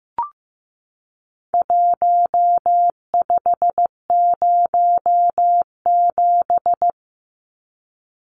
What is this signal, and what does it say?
Morse code, heard as a keyed tone, about 15 wpm, 700 Hz. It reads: 1507
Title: Numbers - Contest Style - 4-Digit 15wpm